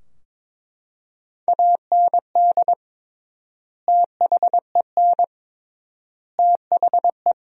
Morse code, heard as a keyed tone, about 21 wpm, 700 Hz. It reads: AND THEN THE
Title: 3x Word-Spacing - Sentences from Top 500 Words - 22wpm